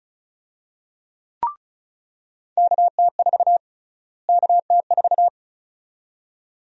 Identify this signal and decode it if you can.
Morse code, heard as a keyed tone, about 36 wpm, 700 Hz. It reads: KT4 KT4
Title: USA Call Sign Prefixes - Repeated 2x - 35wpm